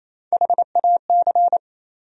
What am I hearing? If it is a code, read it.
Morse code, heard as a keyed tone, about 28 wpm, 700 Hz. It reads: HAC
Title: USA Call Sign Postfixes - 3 Letters 28wpm